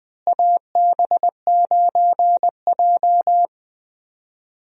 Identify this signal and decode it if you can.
Morse code, heard as a keyed tone, about 20 wpm, 700 Hz. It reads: AB9J
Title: USA Call Signs - 4 Characters - Repeated 2x - 20wpm